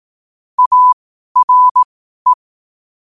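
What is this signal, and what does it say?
Morse code, heard as a keyed tone, about 18 wpm, spaced as for 12 wpm, 1000 Hz. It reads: ARE
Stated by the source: Morse code sent at 12 words per minute.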